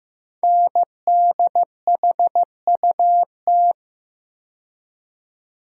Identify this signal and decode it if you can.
Morse code, heard as a keyed tone, about 15 wpm, 700 Hz. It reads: NDHUT
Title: Code Groups - 5 Letters - 15wpm